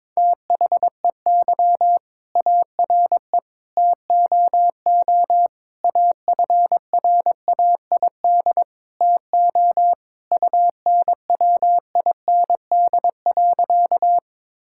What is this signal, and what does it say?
Morse code, heard as a keyed tone, about 21 wpm, 700 Hz. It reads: THEY ARE TOO AFRAID TO UNWIND.
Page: morse-code-ninja-practice-22wpm